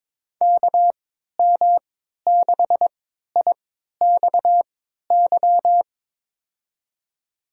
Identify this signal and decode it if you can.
Morse code, heard as a keyed tone, about 22 wpm, spaced as for 12 wpm, 700 Hz. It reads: KM6IXY
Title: USA Call Signs - 3x Character Spacing - 22wpm